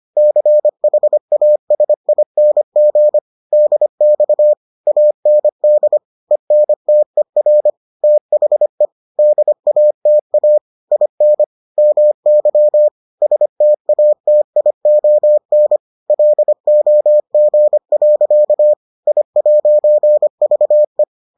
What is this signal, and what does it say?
Morse code, heard as a keyed tone, about 25 wpm, 600 Hz. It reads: CHASING DX AND ENTER THE DATA IN MY STATION LOG. I'VE